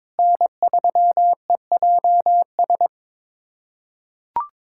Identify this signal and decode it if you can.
Morse code, heard as a keyed tone, about 22 wpm, 700 Hz. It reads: N3EJS
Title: USA Callsign DE Callsign 22wpm